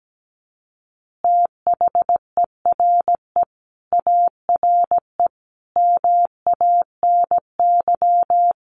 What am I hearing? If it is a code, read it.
Morse code, heard as a keyed tone, about 17 wpm, 700 Hz. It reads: THERE ARE MANY
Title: Easy Phrases 17wpm